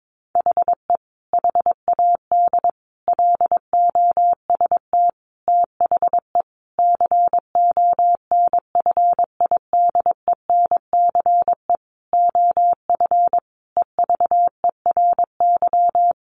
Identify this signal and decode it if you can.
Morse code, heard as a keyed tone, about 22 wpm, 700 Hz. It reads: HE HAD LOST THE CONFIDENCE OF EVERY
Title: Sentences from Top 1500 Words 22wpm